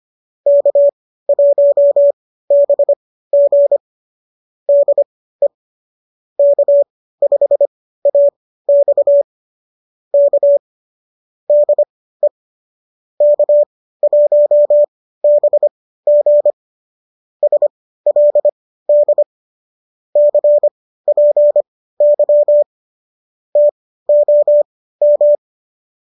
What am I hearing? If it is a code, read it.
Morse code, heard as a keyed tone, about 25 wpm, spaced as for 15 wpm, 590 Hz. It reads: K1BG DE K5AX K DE K1BG SLD CPY TOM